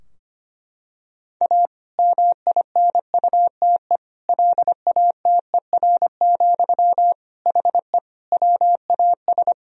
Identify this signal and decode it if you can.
Morse code, heard as a keyed tone, about 25 wpm, 700 Hz. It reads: A MINUTE LATER, HE WAS
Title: Sentences from Top 1000 Words - Review Part 4 - 25wpm